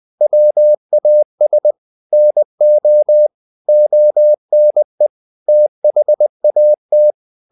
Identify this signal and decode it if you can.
Morse code, heard as a keyed tone, about 20 wpm, 600 Hz. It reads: WAS NO ONE THAT
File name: the old man and the sea / the old man and the sea part 1 20wpm